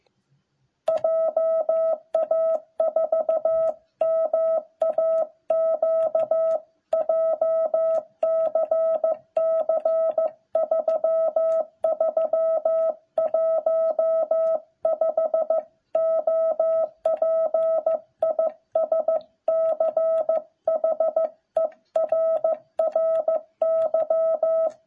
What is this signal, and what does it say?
Morse code, heard as a keyed tone, about 15 wpm, 660 Hz. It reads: JA4MAQ JCC 3315 OP IS CHERRY